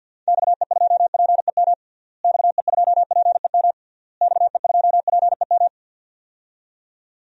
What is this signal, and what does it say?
Morse code, heard as a keyed tone, about 46 wpm, 700 Hz. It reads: XE2PEM XE2PEM XE2PEM
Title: DXCC-100 Call Signs - 6 Characters - Repeated 3x - 50wpm